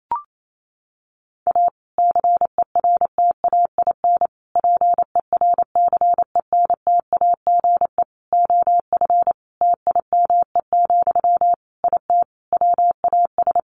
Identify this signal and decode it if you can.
Morse code, heard as a keyed tone, about 28 wpm, 700 Hz. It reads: A CERTAIN PERCENTAGE OF TIME, IT WAS
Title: Sentences from Top 1300 Words 28wpm